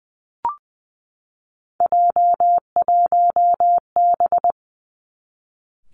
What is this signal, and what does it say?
Morse code, heard as a keyed tone, about 20 wpm, 700 Hz. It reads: J1B
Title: Three Challenging Letter-Number Combinations 20wpm